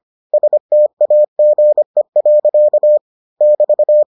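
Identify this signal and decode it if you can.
Morse code, heard as a keyed tone, about 25 wpm, 600 Hz. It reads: STAGE. <BT>
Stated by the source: Morse edition of the Foundations of Amateur Radio episodes.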